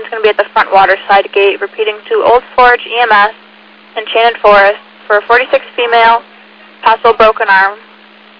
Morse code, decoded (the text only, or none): none